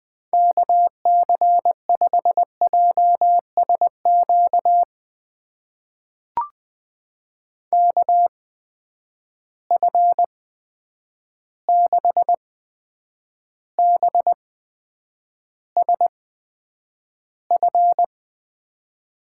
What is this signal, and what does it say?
Morse code, heard as a keyed tone, about 20 wpm, spaced as for 9 wpm, 700 Hz. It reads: KC5JSQ KF6BSF
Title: USA Call Signs - 6 Characters - 8x Character Spacing - 20wpm